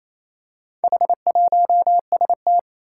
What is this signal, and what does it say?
Morse code, heard as a keyed tone, about 27 wpm, 700 Hz. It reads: H1ST
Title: Four Challenging Letter-Number Combinations 28wpm